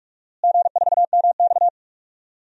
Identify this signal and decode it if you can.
Morse code, heard as a keyed tone, about 46 wpm, 700 Hz. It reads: G4MX
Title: DXCC-100 Call Signs - 4 Characters - Repeated 3x - 45wpm